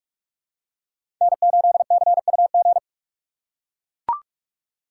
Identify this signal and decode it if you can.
Morse code, heard as a keyed tone, about 43 wpm, 700 Hz. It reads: N8CUG
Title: USA Call Signs - 5 Characters - 6x Character Spacing - 45wpm